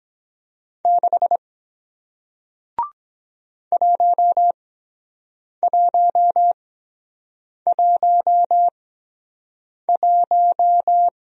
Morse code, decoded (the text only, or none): 6 1111